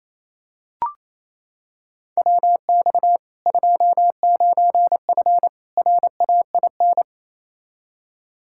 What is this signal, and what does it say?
Morse code, heard as a keyed tone, about 28 wpm, 700 Hz. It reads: WX 29F RAIN
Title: K1USN SST QSO Practice 28wpm